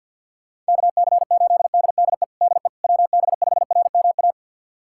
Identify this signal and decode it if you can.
Morse code, heard as a keyed tone, about 52 wpm, 700 Hz. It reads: KC8DDE DE WB5RMA
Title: USA Callsign DE Callsign - 6 Characters - 50wpm